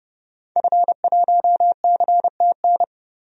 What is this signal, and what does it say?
Morse code, heard as a keyed tone, about 31 wpm, 700 Hz. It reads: F1CTN
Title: DXCC-100 Call Signs - 5 Characters - 30wpm